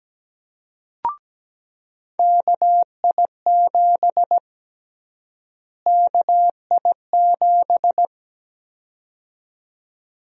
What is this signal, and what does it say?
Morse code, heard as a keyed tone, about 17 wpm, 700 Hz. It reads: KI7 KI7
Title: USA Call Sign Prefixes - Repeated 2x - 17wpm